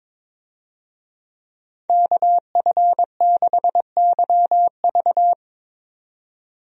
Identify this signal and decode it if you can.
Morse code, heard as a keyed tone, about 22 wpm, 700 Hz. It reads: KF6YV
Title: USA Call Signs - 5 Characters - 22wpm